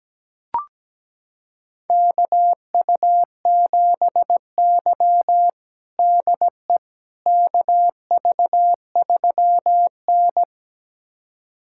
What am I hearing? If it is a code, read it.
Morse code, heard as a keyed tone, about 17 wpm, 700 Hz. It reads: KU7Y DE KV3N
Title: USA Callsign DE Callsign - 4 Characters - 17wpm